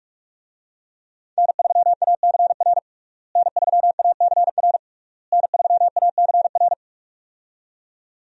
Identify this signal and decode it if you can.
Morse code, heard as a keyed tone, about 41 wpm, 700 Hz. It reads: N3ACR N3ACR N3ACR
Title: USA Call Signs - 5 Characters - Repeated 3x - 45wpm